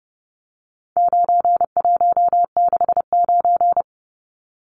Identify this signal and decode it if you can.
Morse code, heard as a keyed tone, about 30 wpm, 700 Hz. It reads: 9169
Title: Numbers - 4-Digit 30wpm